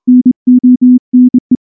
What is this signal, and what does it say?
Morse code, heard as a keyed tone, about 24 wpm, 250 Hz. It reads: NONE
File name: morse / morse-training-25WPM